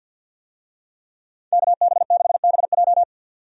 Morse code, heard as a keyed tone, about 50 wpm, 700 Hz. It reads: KB6BJ
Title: USA Call Signs - 5 Characters - 50wpm